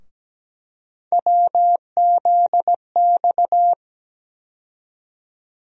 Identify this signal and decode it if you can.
Morse code, heard as a keyed tone, about 17 wpm, 700 Hz. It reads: WZX
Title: USA Call Sign Postfixes - 3 Letters 17wpm